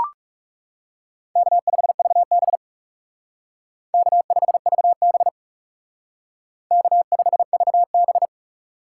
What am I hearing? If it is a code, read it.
Morse code, heard as a keyed tone, about 35 wpm, 700 Hz. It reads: K5VB K5VB K5VB
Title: Speed Racing - USA 1x2 Call Signs 30wpm